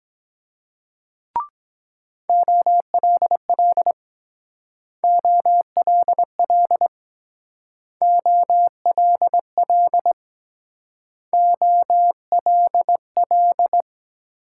Morse code, heard as a keyed tone, about 23 wpm, 700 Hz. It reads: OLL OLL OLL OLL